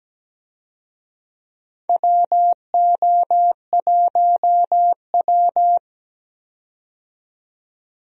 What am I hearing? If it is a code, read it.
Morse code, heard as a keyed tone, about 17 wpm, 700 Hz. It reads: WO1W
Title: USA Call Signs - 4 Characters - 17wpm